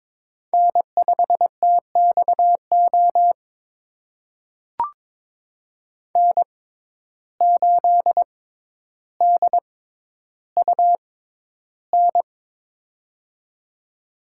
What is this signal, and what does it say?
Morse code, heard as a keyed tone, about 22 wpm, spaced as for 11 wpm, 700 Hz. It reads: N5TXO N8DUN
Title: USA Call Signs - 5 Characters - 6x Character Spacing - 22wpm